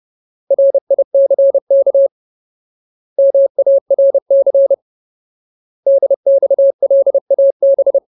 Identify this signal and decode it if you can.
Morse code, heard as a keyed tone, about 30 wpm, 550 Hz. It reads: RICK MARC DXLAB
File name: Ultimate Word List / 30WPM Ultimate